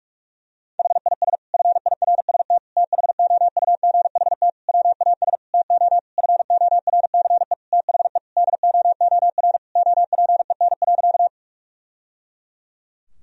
Morse code, heard as a keyed tone, about 46 wpm, 700 Hz. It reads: HIS FIRST THOUGHT WAS TO FORCE THE DOOR OPEN.
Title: Sentences from Top 1000 Words - Review Part 3 - 45wpm